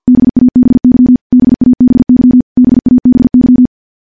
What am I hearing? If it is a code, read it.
Morse code, heard as a keyed tone, about 50 wpm, 250 Hz. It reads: BABY BABY BABY
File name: morse / morse-training-50WPM